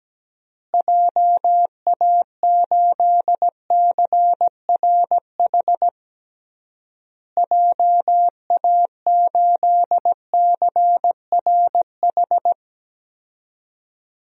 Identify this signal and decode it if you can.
Morse code, heard as a keyed tone, about 17 wpm, 700 Hz. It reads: JA8CRH JA8CRH
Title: DXCC-100 Call Signs - 6 Characters - Repeated 3x - 17wpm